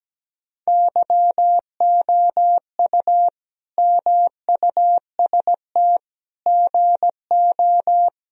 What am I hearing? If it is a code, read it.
Morse code, heard as a keyed tone, about 17 wpm, 700 Hz. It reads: YOU MUST GO
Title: Sentences from Top 1000 Words - Review Part 1 - 17wpm